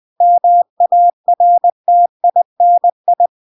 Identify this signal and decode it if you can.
Morse code, heard as a keyed tone, about 20 wpm, 700 Hz. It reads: MARTINI
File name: 20wpm